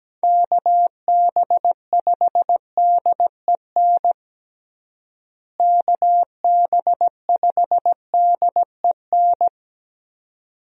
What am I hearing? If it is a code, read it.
Morse code, heard as a keyed tone, about 17 wpm, 700 Hz. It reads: KB5DEN KB5DEN
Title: USA Call Signs - 6 Characters - Repeated 2x - 17wpm